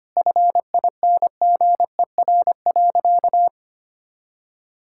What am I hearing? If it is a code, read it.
Morse code, heard as a keyed tone, about 25 wpm, 700 Hz. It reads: FINGER.
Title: Sentences from Top 1600 Words 25wpm